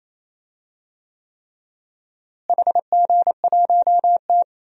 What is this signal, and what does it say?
Morse code, heard as a keyed tone, about 28 wpm, 700 Hz. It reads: HG1T
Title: DXCC-100 Call Signs - 4 Characters - Repeated 3x - 28wpm